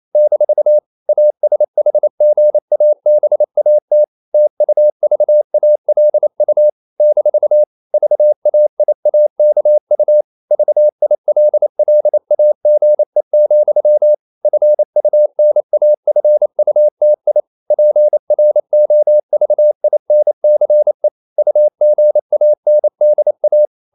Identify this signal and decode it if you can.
Morse code, heard as a keyed tone, about 28 wpm, 600 Hz. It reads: - ASHGABAT TUVALU - VAIAKU VILLAGE, FUNAFUTI PROVINCE UGANDA
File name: CW / Practice / NationalCapitals28